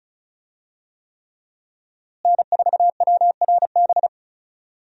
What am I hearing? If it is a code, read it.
Morse code, heard as a keyed tone, about 36 wpm, 700 Hz. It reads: N4WRB
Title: USA Call Signs - 5 Characters - Repeated 3x - 35wpm